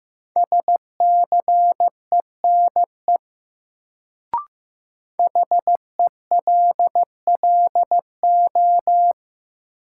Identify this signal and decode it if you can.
Morse code, heard as a keyed tone, about 15 wpm, 700 Hz. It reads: SCENE HELLO
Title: Sentences from Top 1100 Words - New Word Drill - 15wpm